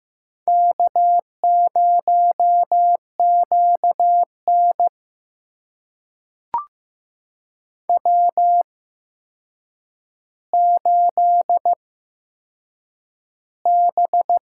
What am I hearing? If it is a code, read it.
Morse code, heard as a keyed tone, about 15 wpm, spaced as for 7 wpm, 700 Hz. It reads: K0QN W8B
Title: USA Call Signs - 4 Characters - 8x Character Spacing - 15wpm